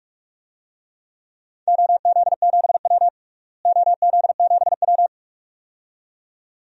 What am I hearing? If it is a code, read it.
Morse code, heard as a keyed tone, about 44 wpm, 700 Hz. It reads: OZ7W OZ7W
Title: DXCC-100 Call Signs - 4 Characters - Repeated 2x - 45wpm